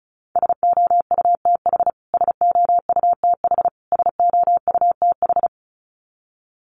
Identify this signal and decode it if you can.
Morse code, heard as a keyed tone, about 34 wpm, 700 Hz. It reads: SOUTH SOUTH SOUTH
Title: Top 500 Words Repeated 35wpm